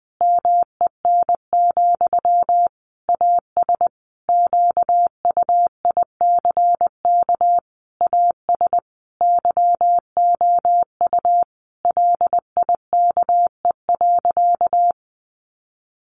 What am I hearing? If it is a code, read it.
Morse code, heard as a keyed tone, about 20 wpm, 700 Hz. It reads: MEN, AS QUICK AS YOU LIKE.
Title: Sentences from Top 500 Words 20wpm